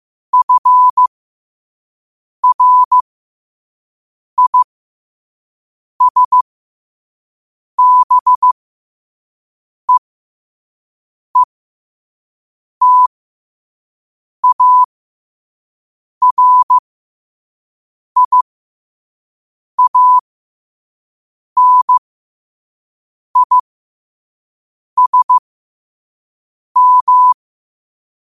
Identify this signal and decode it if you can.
Morse code, heard as a keyed tone, about 15 wpm, spaced as for 5 wpm, 1000 Hz. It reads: FRISBEETARIANISM
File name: qotdfiles / 5wpm